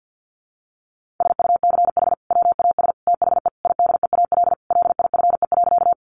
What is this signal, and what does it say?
Morse code, heard as a keyed tone, about 50 wpm, 700 Hz. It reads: SUCH WAS THE IDEAL LIFE.
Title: Sentences from Top 1400 Words 50wpm